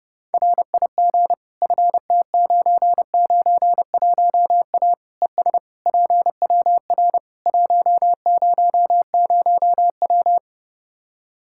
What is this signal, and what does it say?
Morse code, heard as a keyed tone, about 30 wpm, 700 Hz. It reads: RIG FT991A ES PWR 100W
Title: Practice QSO - Exchange 2 - Info - 30wpm